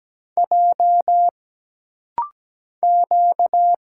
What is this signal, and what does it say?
Morse code, heard as a keyed tone, about 17 wpm, 700 Hz. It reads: J Q
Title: Single Letters - Rapid-Fire 17wpm